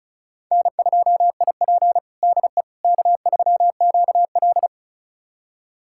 Morse code, heard as a keyed tone, about 36 wpm, 700 Hz. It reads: N2IP DE K3QL
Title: USA Callsign DE Callsign - 4 Characters - 35wpm